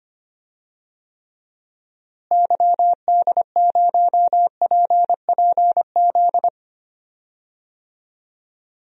Morse code, heard as a keyed tone, about 25 wpm, 700 Hz. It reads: YD0PPZ
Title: DXCC-100 Call Signs - 6 Characters - Repeated 3x - 25wpm